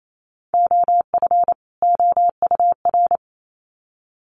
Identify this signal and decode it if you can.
Morse code, heard as a keyed tone, about 28 wpm, 700 Hz. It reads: OF OUR
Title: Sets of 2 Words 28wpm